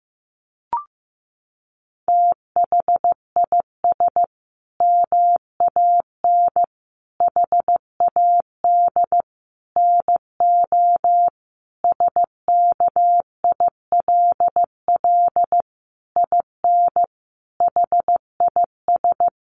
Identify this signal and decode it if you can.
Morse code, heard as a keyed tone, about 15 wpm, 700 Hz. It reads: THIS MAN HAD NO SKILL IN HIS
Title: Sentences from Top 1200 Words 15wpm